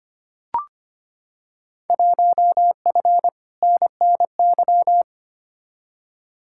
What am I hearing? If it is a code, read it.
Morse code, heard as a keyed tone, about 25 wpm, 700 Hz. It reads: 1F NNY
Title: Field Day Exchanges - 25wpm